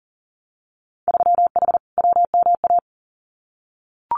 Morse code, heard as a keyed tone, about 40 wpm, 700 Hz. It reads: 3H WMA
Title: Winter Field Day Exchanges Encore 40wpm